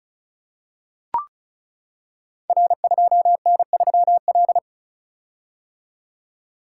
Morse code, heard as a keyed tone, about 36 wpm, 690 Hz. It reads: R2N3L
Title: Code Groups - 5 Letter:Numbers - 35wpm